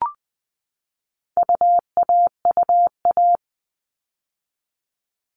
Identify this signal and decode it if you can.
Morse code, heard as a keyed tone, about 20 wpm, 700 Hz. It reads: UAUA
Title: All Cut Numbers - 4-Digit 20wpm